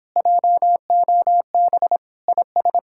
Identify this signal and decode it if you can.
Morse code, heard as a keyed tone, about 26 wpm, 700 Hz. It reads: JOB IS